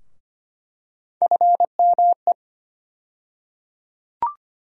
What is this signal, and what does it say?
Morse code, heard as a keyed tone, about 25 wpm, 700 Hz. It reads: FME
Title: USA Call Sign Postfixes - 3 Letters 25wpm